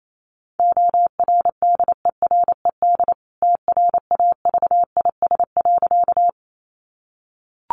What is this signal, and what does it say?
Morse code, heard as a keyed tone, about 28 wpm, 700 Hz. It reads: ORDERED TRAVIS.
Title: A Sound Of Thunder 28wpm